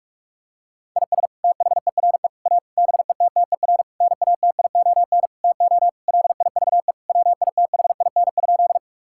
Morse code, heard as a keyed tone, about 46 wpm, 700 Hz. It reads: IS THERE A BETTER NATION TO LIVE WITHIN?
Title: Sentences from Top 1000 Words - Review Part 6 - 45wpm